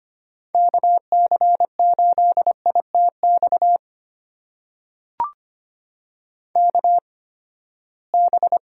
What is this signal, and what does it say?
Morse code, heard as a keyed tone, about 25 wpm, spaced as for 17 wpm, 700 Hz. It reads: KC8ITX KB
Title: USA Call Signs - 6 Characters - 8x Character Spacing - 25wpm